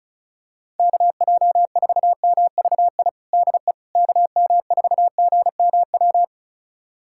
Morse code, heard as a keyed tone, about 36 wpm, 700 Hz. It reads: KJ4MVI DE KM4GMW
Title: USA Callsign DE Callsign - 6 Characters - 35wpm